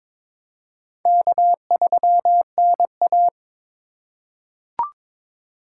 Morse code, heard as a keyed tone, about 22 wpm, 700 Hz. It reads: K3NA
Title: USA Call Signs - 4 Characters - 3x Character Spacing - 22wpm